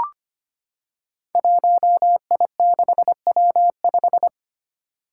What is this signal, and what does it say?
Morse code, heard as a keyed tone, about 25 wpm, 700 Hz. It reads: 1I6W5
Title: Five Challenging Letter-Number Combinations 25wpm